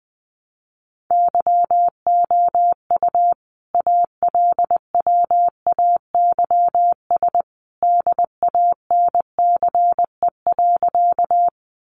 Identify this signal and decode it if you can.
Morse code, heard as a keyed tone, about 20 wpm, 700 Hz. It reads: YOU ALWAYS DANCE.
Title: Sentences from Top 700 Words 20wpm